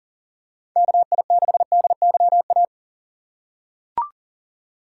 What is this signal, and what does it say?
Morse code, heard as a keyed tone, about 40 wpm, 700 Hz. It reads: KI6DYA
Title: USA Call Signs - 2x Character Spacing - 40wpm